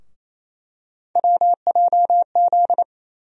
Morse code, heard as a keyed tone, about 28 wpm, 700 Hz. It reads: WJZ
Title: USA Call Sign Postfixes - 3 Letters 28wpm